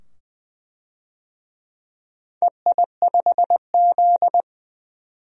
Morse code, heard as a keyed tone, about 20 wpm, 700 Hz. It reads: EI5Z
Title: DXCC-100 Call Signs - 4 Characters - 20wpm